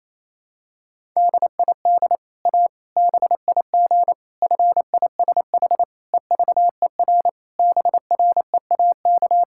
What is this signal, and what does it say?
Morse code, heard as a keyed tone, about 28 wpm, 700 Hz. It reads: DID A BIG FISH EVER BREAK
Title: Sentences from Top 1000 Words - Review Part 6 - 28wpm